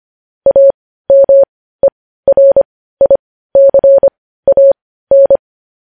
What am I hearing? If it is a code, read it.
Morse code, heard as a keyed tone, about 25 wpm, spaced as for 15 wpm, 550 Hz. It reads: AMERICAN
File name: podcast / News headlines-1x-25x15